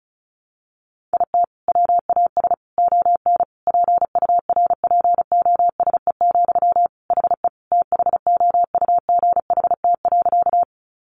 Morse code, heard as a keyed tone, about 34 wpm, 700 Hz. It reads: IT WAS ON PURPOSE, HE THOUGHT.
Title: Sentences from Top 800 Words 35wpm